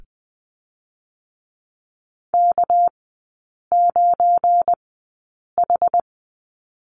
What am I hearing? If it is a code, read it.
Morse code, heard as a keyed tone, about 20 wpm, spaced as for 8 wpm, 700 Hz. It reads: K9H
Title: 205 Lesson 33e - Letter j - Three Characters 20wpm